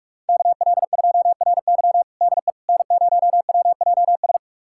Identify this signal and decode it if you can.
Morse code, heard as a keyed tone, about 46 wpm, 700 Hz. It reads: KL2RY DE N0WJS
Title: USA Callsign DE Callsign - 5 Characters - 45wpm